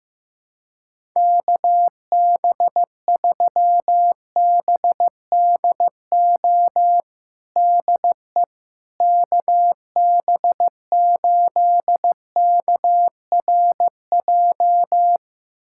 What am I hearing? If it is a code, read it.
Morse code, heard as a keyed tone, about 15 wpm, 700 Hz. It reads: KB3BDO DE KB8KRJ
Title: USA Callsign DE Callsign 15wpm